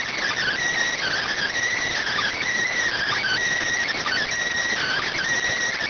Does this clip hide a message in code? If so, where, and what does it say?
Yes: Emergency Alert System (SAME) tones (header bursts) from 0.9 to 5.2 s, heard as harsh buzzing bursts.